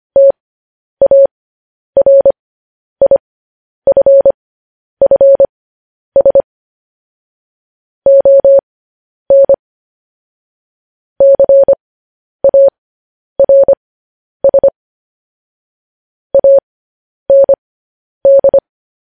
Morse code, heard as a keyed tone, about 25 wpm, spaced as for 10 wpm, 550 Hz. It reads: TARIFFS ON CARS AND
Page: News Headlines in Morse Code at 10 WPM